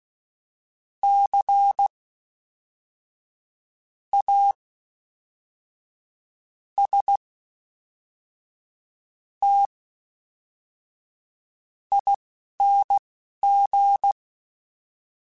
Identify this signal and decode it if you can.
Morse code, heard as a keyed tone, about 16 wpm, spaced as for 5 wpm, 800 Hz. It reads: CASTING
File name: Special INGs / Special INGs-mp3s / Fast ing @10